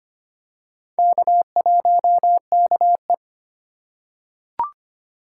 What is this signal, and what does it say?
Morse code, heard as a keyed tone, about 25 wpm, 700 Hz. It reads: K1KE
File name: practice / 25wpm / Four Letter-Number Combinations - 25wpm